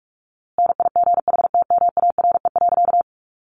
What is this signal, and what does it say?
Morse code, heard as a keyed tone, about 44 wpm, 700 Hz. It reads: NIGHTMARE.
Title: A Sound Of Thunder 45wpm